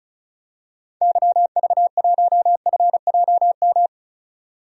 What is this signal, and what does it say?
Morse code, heard as a keyed tone, about 36 wpm, 700 Hz. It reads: YV1FJM
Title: DXCC-100 Call Signs - 6 Characters - 35wpm